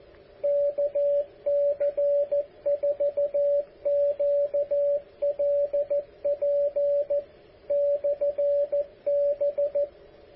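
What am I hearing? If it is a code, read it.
Morse code, heard as a keyed tone, about 14 wpm, 570 Hz, with radio noise. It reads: KC4QLP/B
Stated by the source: KC4QLP 10 meter beacon, transmitter built by AA7DJ, power just over 10 watts.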